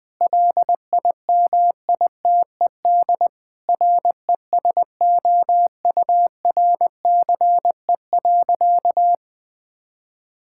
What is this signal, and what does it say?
Morse code, heard as a keyed tone, about 20 wpm, 700 Hz. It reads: LIMITED RESOURCE.
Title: Difficult Sentences with Hints - 20wpm